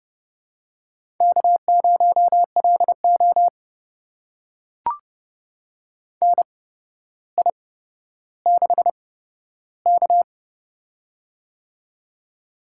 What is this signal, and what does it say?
Morse code, heard as a keyed tone, about 30 wpm, spaced as for 13 wpm, 700 Hz. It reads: K0LO NI6K
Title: USA Call Signs - 4 Characters - 8x Character Spacing - 30wpm